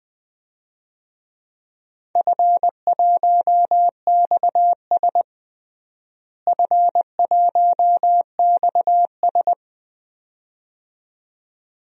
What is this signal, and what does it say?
Morse code, heard as a keyed tone, about 20 wpm, 700 Hz. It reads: F1XS F1XS
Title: DXCC-100 Call Signs - 4 Characters - Repeated 2x - 20wpm